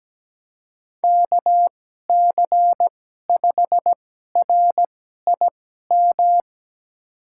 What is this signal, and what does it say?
Morse code, heard as a keyed tone, about 17 wpm, spaced as for 12 wpm, 700 Hz. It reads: KC5RIM
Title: USA Call Signs - 6 Characters - 2x Character Spacing - 17wpm